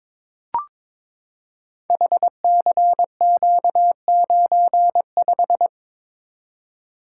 Morse code, heard as a keyed tone, about 22 wpm, 700 Hz. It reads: HCQ95
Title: Code Groups - 5 Letter:Numbers - 22wpm